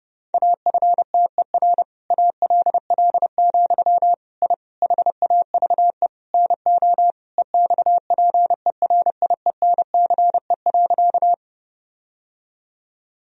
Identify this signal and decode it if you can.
Morse code, heard as a keyed tone, about 30 wpm, 700 Hz. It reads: AFTER ALL, I HAVE NO EXPERIENCE.
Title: Sentences from Top 1000 Words - Review Part 3 - 30wpm